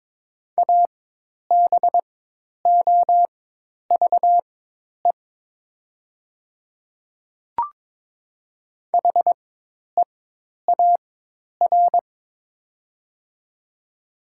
Morse code, heard as a keyed tone, about 22 wpm, spaced as for 10 wpm, 700 Hz. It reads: ABOVE HEAR
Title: 4x Character Spacing - Top 500 Words - 22wpm